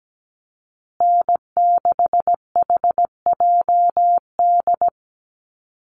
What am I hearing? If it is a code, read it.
Morse code, heard as a keyed tone, about 17 wpm, 700 Hz. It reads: N6HJD
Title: USA Call Signs Repeated 3x 17wpm